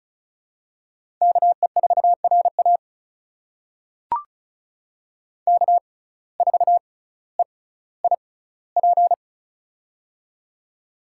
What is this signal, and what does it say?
Morse code, heard as a keyed tone, about 36 wpm, spaced as for 18 wpm, 700 Hz. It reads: KE4RA K4EIP
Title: USA Call Signs - 5 Characters - 6x Character Spacing - 35wpm